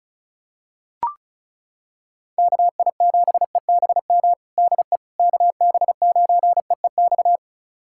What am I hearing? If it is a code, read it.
Morse code, heard as a keyed tone, about 34 wpm, 700 Hz. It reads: KI7EBM DE KB9EEX
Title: USA Callsign DE Callsign - 6 Characters - 35wpm